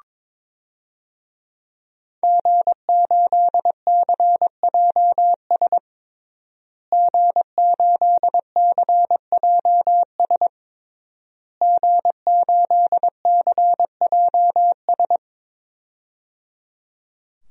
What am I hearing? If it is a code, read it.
Morse code, heard as a keyed tone, about 22 wpm, 700 Hz. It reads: G8CJS G8CJS G8CJS
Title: DXCC-100 Call Signs - 5 Characters - Repeated 3x - 22wpm